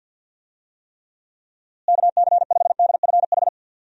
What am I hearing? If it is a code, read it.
Morse code, heard as a keyed tone, about 53 wpm, 700 Hz. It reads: KC5DFH
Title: USA Call Signs - 6 Characters - Repeated 3x - 50wpm